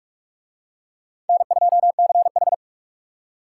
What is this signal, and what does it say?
Morse code, heard as a keyed tone, about 41 wpm, 700 Hz. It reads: N2CH